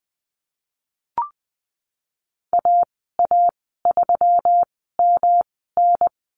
Morse code, heard as a keyed tone, about 20 wpm, spaced as for 14 wpm, 700 Hz. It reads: AA3MN
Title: 167 Lesson 27g - Number 7 - Call Signs 20wpm